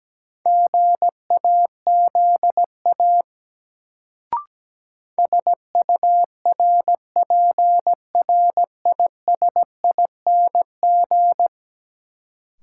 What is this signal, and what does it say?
Morse code, heard as a keyed tone, about 17 wpm, 700 Hz. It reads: GAZA SURPRISING
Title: Sets of 5 Words - New Word Drill - 17wpm